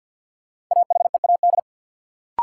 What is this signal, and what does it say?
Morse code, heard as a keyed tone, about 53 wpm, 700 Hz. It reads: AHEAD
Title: Sentences from Top 1000 Words - Review Part 6 - New Word Drill - 50wpm